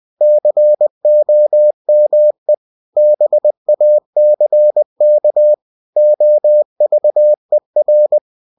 Morse code, heard as a keyed tone, about 20 wpm, 600 Hz. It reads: COME BACK OVER
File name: cw-books / winnie-the-pooh / winnie-20wpm-page-0101